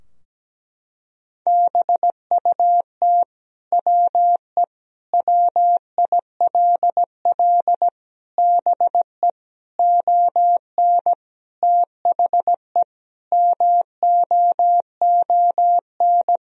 Morse code, heard as a keyed tone, about 17 wpm, 700 Hz. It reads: BUT WE WILL BE ON THE MOON
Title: Sentences from Top 1000 Words - Review Part 3 - 17wpm